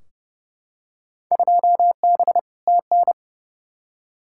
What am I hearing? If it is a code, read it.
Morse code, heard as a keyed tone, about 32 wpm, 700 Hz. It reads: 2B TN
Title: Field Day Exchanges Encore - 30wpm